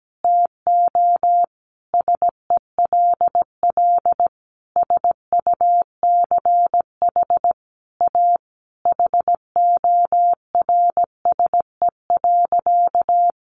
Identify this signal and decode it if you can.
Morse code, heard as a keyed tone, about 17 wpm, 700 Hz. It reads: TO SELL SUCH A HORSE.
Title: Sentences from Top 1100 Words 17wpm